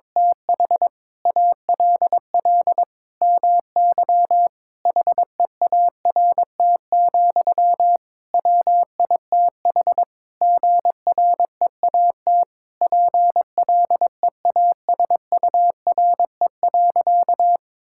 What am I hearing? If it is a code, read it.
Morse code, heard as a keyed tone, about 21 wpm, 700 Hz. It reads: TH ALL MY HEART, WITH GREAT PLEASURE.
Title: Sentences from Top 1600 Words 22wpm